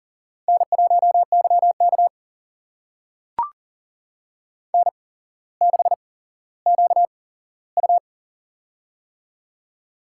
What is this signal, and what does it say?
Morse code, heard as a keyed tone, about 39 wpm, spaced as for 17 wpm, 700 Hz. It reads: N1YK N6QU
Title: USA Call Signs - 4 Characters - 8x Character Spacing - 40wpm